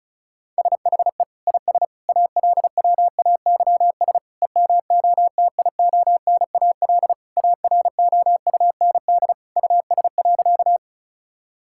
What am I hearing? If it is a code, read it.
Morse code, heard as a keyed tone, about 34 wpm, 700 Hz. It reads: SHE IS ALWAYS EMOTIONAL AROUND US.
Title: Sentences from Top 1000 Words - Review Part 5 - 35wpm